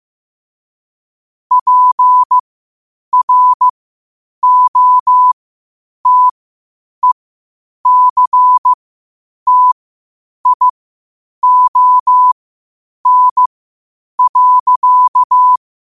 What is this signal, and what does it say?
Morse code, heard as a keyed tone, about 15 wpm, spaced as for 8 wpm, 1000 Hz. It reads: PROTECTION.